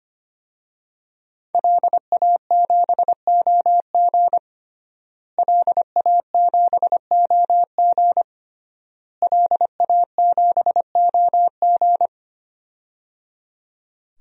Morse code, heard as a keyed tone, about 25 wpm, 700 Hz. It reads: LA7OG LA7OG LA7OG
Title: DXCC-100 Call Signs - 5 Characters - Repeated 3x - 25wpm